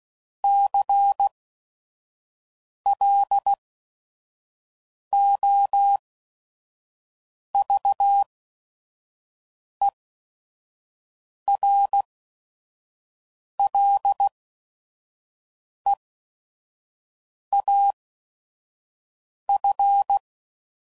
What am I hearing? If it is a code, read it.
Morse code, heard as a keyed tone, about 16 wpm, spaced as for 5 wpm, 800 Hz. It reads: CLOVERLEAF
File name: CW Code Groups @5WPM / fromSM's / Cities